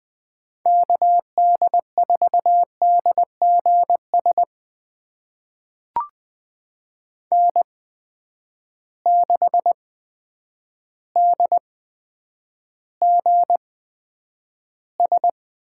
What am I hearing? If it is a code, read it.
Morse code, heard as a keyed tone, about 20 wpm, spaced as for 9 wpm, 700 Hz. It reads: KD4DGS N6DGS
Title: USA Call Signs - 8x Character Spacing - 20wpm